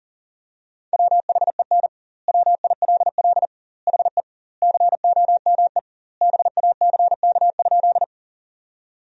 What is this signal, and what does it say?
Morse code, heard as a keyed tone, about 40 wpm, 700 Hz. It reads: WHEN WILL HE COME BACK?
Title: 2x Word-Spacing - Sentences from Top 200 Words - 40wpm